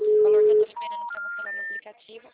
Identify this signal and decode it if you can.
Special information tone, with speech.
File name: out- -2046- - - .47555